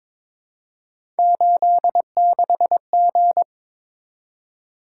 Morse code, heard as a keyed tone, about 22 wpm, 700 Hz. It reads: 86G
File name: practice / 22wpm / Three Letter-Number Combinations - 22wpm